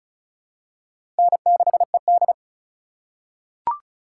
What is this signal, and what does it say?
Morse code, heard as a keyed tone, about 34 wpm, 700 Hz. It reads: N6ED
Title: USA Call Signs - 4 Characters - 4x Character Spacing - 35wpm